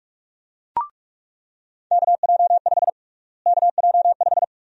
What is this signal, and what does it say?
Morse code, heard as a keyed tone, about 46 wpm, 700 Hz. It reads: KJ5 KJ5
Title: USA Call Sign Prefixes - Repeated 2x - 45wpm